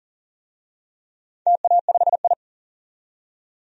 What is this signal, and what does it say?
Morse code, heard as a keyed tone, about 40 wpm, 700 Hz. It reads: TA5I
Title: DXCC-100 Call Signs - 4 Characters - 40wpm